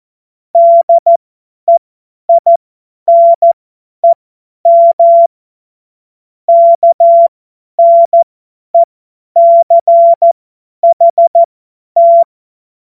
Morse code, heard as a keyed tone, about 14 wpm, spaced as for 10 wpm, 680 Hz. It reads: DEINEM KNECHT
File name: Witz-09-September-2025-50-F70